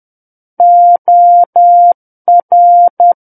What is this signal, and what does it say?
Morse code, heard as a keyed tone, about 10 wpm, 700 Hz. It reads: OR
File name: Book One / 01-Chapter 1 - 10 WPM